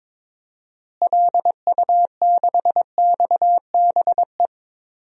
Morse code, heard as a keyed tone, about 22 wpm, 700 Hz. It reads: LU6XBE
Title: DXCC-100 Call Signs - 6 Characters - Repeated 2x - 22wpm